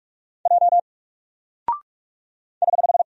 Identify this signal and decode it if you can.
Morse code, heard as a keyed tone, about 44 wpm, 700 Hz. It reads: J <HH>
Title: Single Character (Letter:Number:Punctuation:Prosign) 45wpm